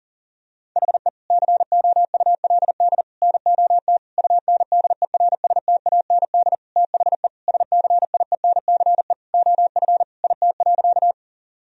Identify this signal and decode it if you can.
Morse code, heard as a keyed tone, about 37 wpm, 700 Hz. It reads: HE COULD NOT UNDERSTAND THE SCIENCE OF IT.
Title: Sentences from Top 1000 Words - Review Part 2 - 40wpm